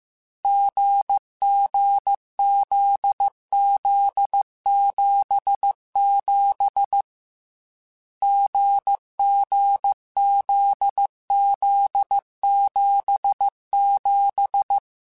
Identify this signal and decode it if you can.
Morse code, heard as a keyed tone, about 15 wpm, 800 Hz. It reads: GGZZ77 GGZZ77
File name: Difficult Combos / Difficult Combos-mp3s / GZ GZ7 @13